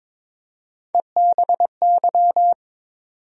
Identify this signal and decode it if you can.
Morse code, heard as a keyed tone, about 22 wpm, 700 Hz. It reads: EBY
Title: USA Call Sign Postfixes - 3 Letters 22wpm